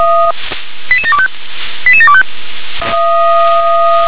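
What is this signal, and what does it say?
Five-tone selective call (ZVEI), heard as quick runs of five tones, tones of about 70 ms. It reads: 88725 88725
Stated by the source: ZVEI five-tone selective call received by SDR.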